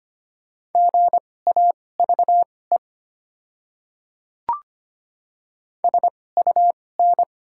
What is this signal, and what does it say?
Morse code, heard as a keyed tone, about 25 wpm, spaced as for 18 wpm, 700 Hz. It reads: GAVE SUN
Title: 2x Character Spacing - Top 500 Words - 25wpm